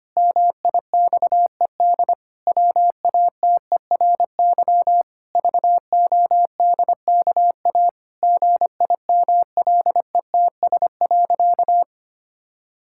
Morse code, heard as a keyed tone, about 25 wpm, 700 Hz. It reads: MIXED WATERY VODKA GIMLETS.
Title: Pangram Challenge v2 - 25wpm